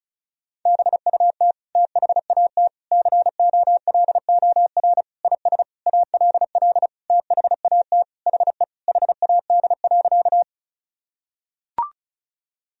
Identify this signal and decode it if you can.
Morse code, heard as a keyed tone, about 36 wpm, 700 Hz. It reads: BUT THAT COLOR IS ALL THAT HE HAD.
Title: Sentences from Top 1000 Words - Review Part 2 - 35wpm